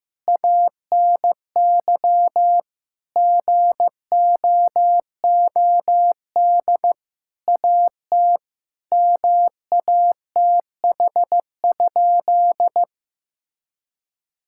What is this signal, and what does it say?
Morse code, heard as a keyed tone, about 16 wpm, 700 Hz. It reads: ANY GOOD AT MATH?
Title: Sentences from Top 1000 Words - Review Part 6 - 15wpm